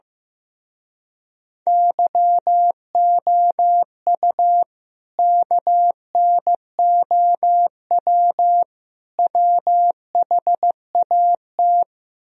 Morse code, heard as a keyed tone, about 15 wpm, 700 Hz. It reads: YOU KNOW WHAT
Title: Speed-Racing - Sets of 3 Words 15wpm